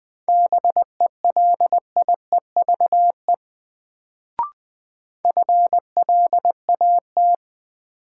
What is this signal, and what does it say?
Morse code, heard as a keyed tone, about 21 wpm, 700 Hz. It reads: BELIEVE FLAT
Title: Sentences from Top 1000 Words - New Word Drill - 20wpm